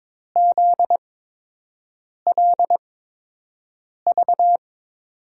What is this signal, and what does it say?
Morse code, heard as a keyed tone, about 22 wpm, spaced as for 6 wpm, 700 Hz. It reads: ZLV
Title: USA Call Signs - 6 Characters - 8x Character Spacing - 22wpm